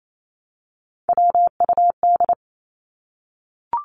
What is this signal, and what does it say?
Morse code, heard as a keyed tone, about 28 wpm, 700 Hz. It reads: WUD
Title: Abbreviations Repeated 3x - 28wpm